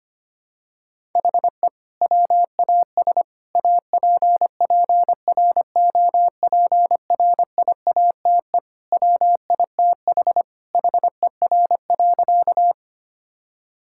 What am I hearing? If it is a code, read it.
Morse code, heard as a keyed tone, about 25 wpm, 700 Hz. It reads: HE WAS APPROPRIATE WITH HER.
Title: Sentences from Top 1000 Words - Review Part 5 - 25wpm